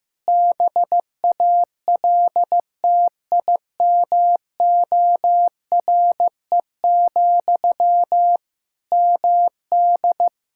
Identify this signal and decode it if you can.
Morse code, heard as a keyed tone, about 15 wpm, 700 Hz. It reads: BALTIMORE, MD
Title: Practice QSO - Exchange 1 - Beginning - 15wpm